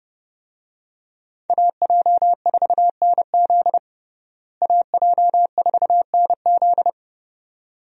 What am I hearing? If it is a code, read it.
Morse code, heard as a keyed tone, about 30 wpm, 700 Hz. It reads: AJ4NZ AJ4NZ
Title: USA Call Signs - 5 Characters - Repeated 2x - 30wpm